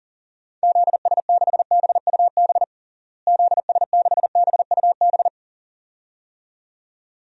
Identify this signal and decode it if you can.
Morse code, heard as a keyed tone, about 40 wpm, 700 Hz. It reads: ZS6BUB ZS6BUB
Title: DXCC-100 Call Signs - 6 Characters - Repeated 2x - 40wpm